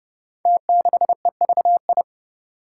Morse code, heard as a keyed tone, about 30 wpm, 700 Hz. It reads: T6EVI
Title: Five Challenging Letter-Number Combinations 30wpm